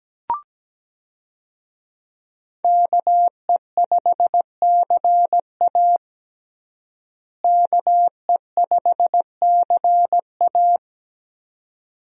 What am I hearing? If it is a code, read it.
Morse code, heard as a keyed tone, about 17 wpm, 700 Hz. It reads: KE5CA KE5CA
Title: USA Call Signs - 5 Characters - Repeated 2x - 17wpm